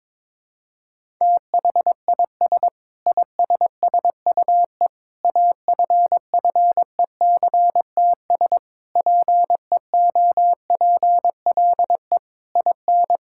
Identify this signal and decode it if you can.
Morse code, heard as a keyed tone, about 22 wpm, 700 Hz. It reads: THIS ISSUE AFFECTS PEOPLE IN
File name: practice / 22wpm / Sentences with 8-Word Phrases - 22wpm